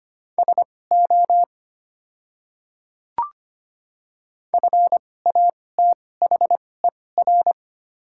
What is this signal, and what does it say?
Morse code, heard as a keyed tone, about 25 wpm, spaced as for 18 wpm, 700 Hz. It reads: SO FATHER
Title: 2x Character Spacing - Top 200 Words - 25wpm